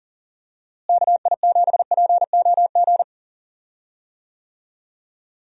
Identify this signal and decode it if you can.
Morse code, heard as a keyed tone, about 37 wpm, 700 Hz. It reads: KI7POG
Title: USA Call Signs - 6 Characters - Repeated 3x - 40wpm